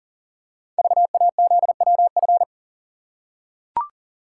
Morse code, heard as a keyed tone, about 42 wpm, 700 Hz. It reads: VAZWF
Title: Code Groups - 5 Letter:Numbers - 40wpm